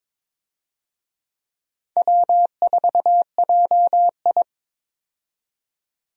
Morse code, heard as a keyed tone, about 22 wpm, 700 Hz. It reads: W4JI
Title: USA Call Signs - 4 Characters - 22wpm